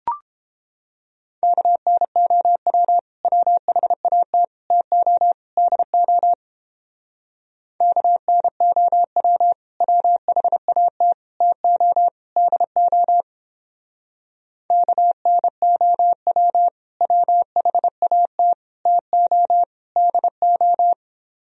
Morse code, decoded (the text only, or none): KNOW WHAT TO DO KNOW WHAT TO DO KNOW WHAT TO DO